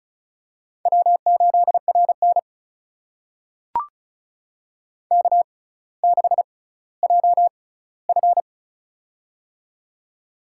Morse code, heard as a keyed tone, about 36 wpm, spaced as for 18 wpm, 700 Hz. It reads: W8RN K6JF
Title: USA Call Signs - 4 Characters - 6x Character Spacing - 35wpm